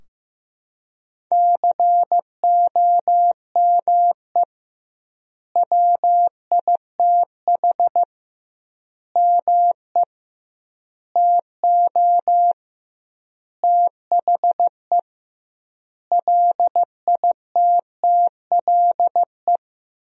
Morse code, heard as a keyed tone, about 15 wpm, 700 Hz. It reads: COME WITH ME TO THE LITTLE
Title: 2x Word-Spacing - Sentences from Top 200 Words - 15wpm